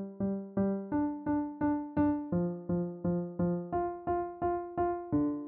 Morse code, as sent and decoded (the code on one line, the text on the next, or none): none